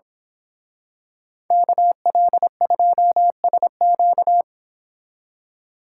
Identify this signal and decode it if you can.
Morse code, heard as a keyed tone, about 26 wpm, 700 Hz. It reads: KL2SQ
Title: Speed Racing - USA 2x2 Call Signs 22wpm